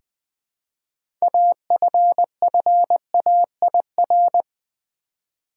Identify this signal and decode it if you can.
Morse code, heard as a keyed tone, about 20 wpm, 700 Hz. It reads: AFFAIR
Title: Six Letter Words - 20wpm